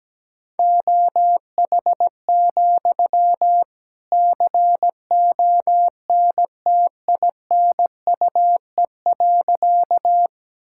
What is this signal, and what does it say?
Morse code, heard as a keyed tone, about 17 wpm, 700 Hz. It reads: OH, CONTINUE.
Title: Sentences from Top 1000 Words - Review Part 6 - 17wpm